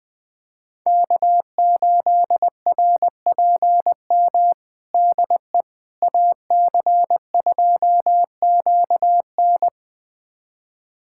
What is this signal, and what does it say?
Morse code, heard as a keyed tone, about 21 wpm, 700 Hz. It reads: K8RPM DE AC2QN
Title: USA Callsign DE Callsign - 5 Characters - 20wpm